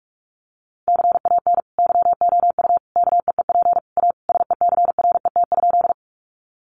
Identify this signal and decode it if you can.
Morse code, heard as a keyed tone, about 44 wpm, 700 Hz. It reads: CAN YOU KEEP A SECRET?
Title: Sentences from Top 1200 Words 45wpm